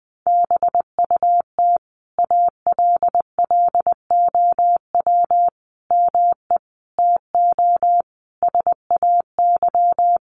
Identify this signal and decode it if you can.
Morse code, heard as a keyed tone, about 20 wpm, 700 Hz. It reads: BUT ALLOW ME TO SAY
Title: Sentences from Top 700 Words 20wpm